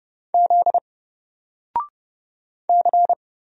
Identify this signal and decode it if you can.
Morse code, heard as a keyed tone, about 30 wpm, 700 Hz. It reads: Z C
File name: Single-Letter-Number-30wpm